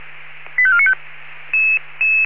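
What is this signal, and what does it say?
Five-tone selective call (ZVEI), heard as quick runs of five tones, tones of about 70 ms. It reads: 85495
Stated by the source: ZVEI five-tone selective call received by SDR.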